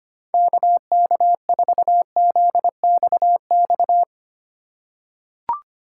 Morse code, heard as a keyed tone, about 25 wpm, 700 Hz. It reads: KK4ZXX
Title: USA Call Signs - 6 Characters - 3x Character Spacing - 25wpm